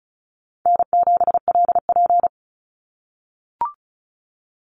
Morse code, heard as a keyed tone, about 34 wpm, 700 Hz. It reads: N7LP
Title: NA SOTA Activator Callsigns 35wpm